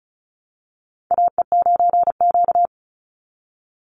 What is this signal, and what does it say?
Morse code, heard as a keyed tone, about 34 wpm, 700 Hz. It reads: AE9Q
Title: NA SOTA Chaser Callsigns 35wpm